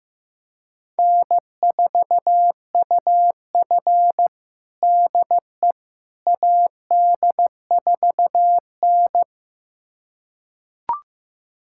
Morse code, heard as a keyed tone, about 15 wpm, 700 Hz. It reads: N4UF DE AD4N
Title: USA Callsign DE Callsign - 4 Characters - 15wpm